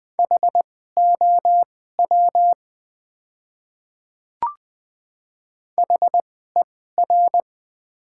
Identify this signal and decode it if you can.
Morse code, heard as a keyed tone, about 20 wpm, spaced as for 14 wpm, 700 Hz. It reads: HOW HER
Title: 2x Character Spacing - Top 200 Words - 20wpm